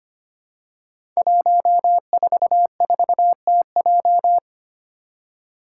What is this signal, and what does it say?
Morse code, heard as a keyed tone, about 25 wpm, 700 Hz. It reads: 144TJ
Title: Five Challenging Letter-Number Combinations 25wpm